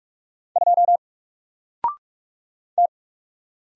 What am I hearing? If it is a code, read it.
Morse code, heard as a keyed tone, about 45 wpm, 700 Hz. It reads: 2 T
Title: Single Character (Letter:Number:Punctuation:Prosign) Alt 45wpm